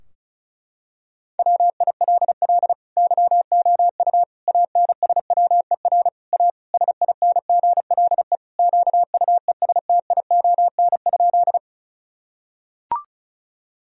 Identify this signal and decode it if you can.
Morse code, heard as a keyed tone, about 36 wpm, 700 Hz. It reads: WILL YOU ANSWER A SINGLE QUESTION?
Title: Sentences from Top 1000 Words - Review Part 4 - 35wpm